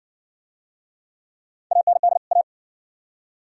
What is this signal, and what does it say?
Morse code, heard as a keyed tone, about 56 wpm, 690 Hz. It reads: AND A